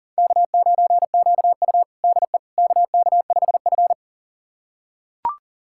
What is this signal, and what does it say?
Morse code, heard as a keyed tone, about 40 wpm, 700 Hz. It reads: K9QU DE KK5F
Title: USA Callsign DE Callsign - 4 Characters - 40wpm